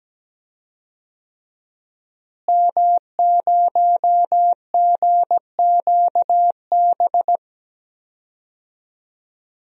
Morse code, heard as a keyed tone, about 17 wpm, 700 Hz. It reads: M0GQB
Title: DXCC-100 Call Signs - 5 Characters - Repeated 3x - 17wpm